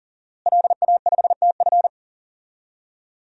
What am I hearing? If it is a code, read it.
Morse code, heard as a keyed tone, about 40 wpm, 700 Hz. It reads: LA5TF
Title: DXCC-100 Call Signs - 5 Characters - Repeated 2x - 40wpm